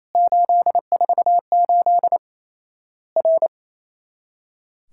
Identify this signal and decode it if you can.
Morse code, heard as a keyed tone, about 28 wpm, 700 Hz. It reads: 848 R
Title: Worked All Europe DX Contest - QTC Exchange - 28wpm